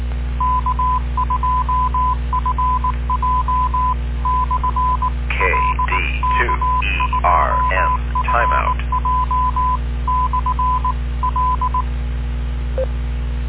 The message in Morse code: K2FJ/L K2FJ/L